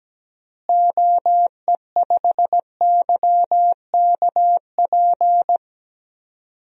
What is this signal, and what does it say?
Morse code, heard as a keyed tone, about 17 wpm, 700 Hz. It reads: OE5YKP
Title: DXCC-100 Call Signs - 6 Characters - Repeated 2x - 17wpm